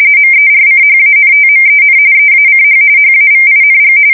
RTTY, about 45.45 baud. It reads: MERCHANTMEN EACH HAVING
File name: fsk170 / battleshipnj